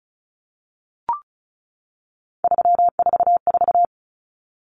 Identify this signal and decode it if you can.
Morse code, heard as a keyed tone, about 34 wpm, 700 Hz. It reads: 344
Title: Numbers - 3-Digit 35wpm